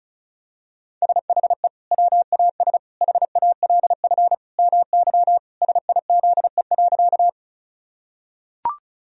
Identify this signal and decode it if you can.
Morse code, heard as a keyed tone, about 34 wpm, 700 Hz. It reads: SHE WAS HALF MY SIZE.
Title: Sentences from Top 1000 Words - Review Part 2 - 35wpm